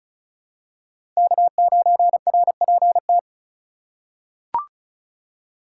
Morse code, heard as a keyed tone, about 34 wpm, 690 Hz. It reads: K9RPT
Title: USA Call Signs - 5 Characters - 2x Character Spacing - 35wpm